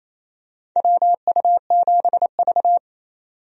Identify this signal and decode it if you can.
Morse code, heard as a keyed tone, about 28 wpm, 700 Hz. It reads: WU7V
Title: USA Call Signs - 4 Characters - 28wpm